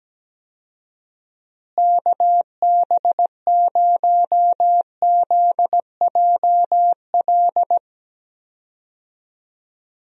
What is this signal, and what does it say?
Morse code, heard as a keyed tone, about 17 wpm, 700 Hz. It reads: KB0ZJL
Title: USA Call Signs - 6 Characters - Repeated 2x - 17wpm